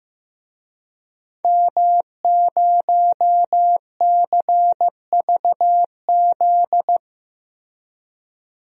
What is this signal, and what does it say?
Morse code, heard as a keyed tone, about 15 wpm, 700 Hz. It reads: M0CVZ
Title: DXCC-100 Call Signs - 5 Characters - 15wpm